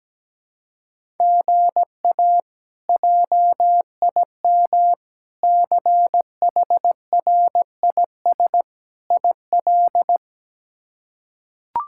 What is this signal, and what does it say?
Morse code, heard as a keyed tone, about 17 wpm, 700 Hz. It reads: GA JIM CHRIS IL
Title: K1USN SST QSO Practice - 17wpm